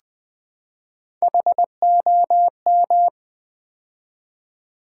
Morse code, heard as a keyed tone, about 20 wpm, 700 Hz. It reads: HOM
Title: Build A Word - 3 Letters At A Time - 8-Letter Words - 20wpm